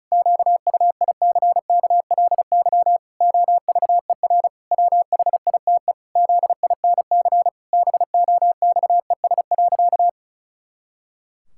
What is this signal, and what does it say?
Morse code, heard as a keyed tone, about 34 wpm, 700 Hz. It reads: QUICKLY OVER WHITE ZINC BOXES.
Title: Pangram Challenge v2 - 35wpm